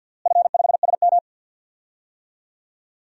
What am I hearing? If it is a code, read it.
Morse code, heard as a keyed tone, about 53 wpm, 700 Hz. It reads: F5SM
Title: DXCC-100 Call Signs - 4 Characters - Repeated 2x - 50wpm